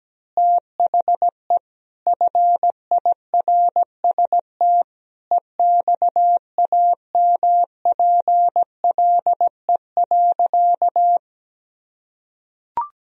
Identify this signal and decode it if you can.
Morse code, heard as a keyed tone, about 17 wpm, 700 Hz. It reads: THE FIRST EXAMPLE.
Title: Sentences from Top 1000 Words - Review Part 1 - 17wpm